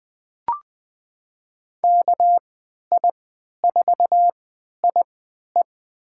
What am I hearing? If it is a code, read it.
Morse code, heard as a keyed tone, about 20 wpm, spaced as for 11 wpm, 700 Hz. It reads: KI4IE
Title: USA Call Signs - 5 Characters - 3x Character Spacing - 20wpm